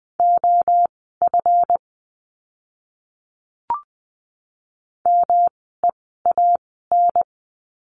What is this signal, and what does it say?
Morse code, heard as a keyed tone, about 20 wpm, spaced as for 15 wpm, 700 Hz. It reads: OF MEAN
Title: 244 Lesson 39f - Letter z - Words 20wpm